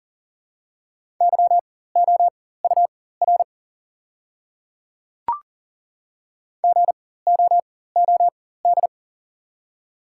Morse code, heard as a keyed tone, about 40 wpm, spaced as for 19 wpm, 690 Hz. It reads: YOUR GOOD
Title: 4x Character Spacing - Top 200 Words - 40wpm